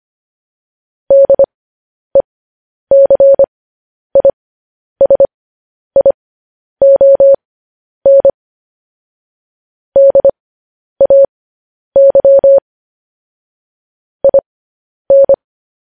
Morse code, heard as a keyed tone, about 25 wpm, spaced as for 10 wpm, 550 Hz. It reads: DECISION DAY IN
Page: news-headlines-in-morse-code-at-10-wpm